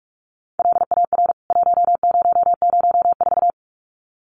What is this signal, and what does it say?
Morse code, heard as a keyed tone, about 44 wpm, 700 Hz. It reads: LAR 1004
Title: CWT Exchange 45wpm